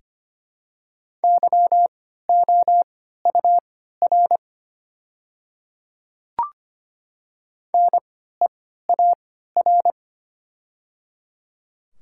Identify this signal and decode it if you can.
Morse code, heard as a keyed tone, about 25 wpm, spaced as for 14 wpm, 700 Hz. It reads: YOUR NEAR
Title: 3x Character Spacing - Top 200 Words - 25wpm